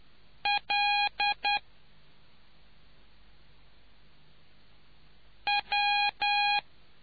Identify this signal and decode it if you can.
Morse code, heard as a keyed tone, about 10 wpm, 820 Hz. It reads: L W